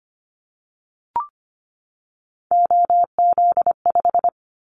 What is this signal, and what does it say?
Morse code, heard as a keyed tone, about 25 wpm, 700 Hz. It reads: OZ5
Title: Callsign Prefixes for Top 50 DXCC - 25wpm